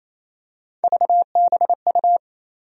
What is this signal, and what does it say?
Morse code, heard as a keyed tone, about 28 wpm, 700 Hz. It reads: VBU
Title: USA Call Sign Postfixes - 3 Letters 28wpm